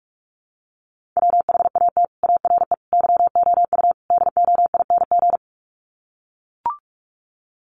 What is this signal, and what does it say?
Morse code, heard as a keyed tone, about 44 wpm, 700 Hz. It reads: WHAT ARE YOU DOING
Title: Sets of 4 Words 45wpm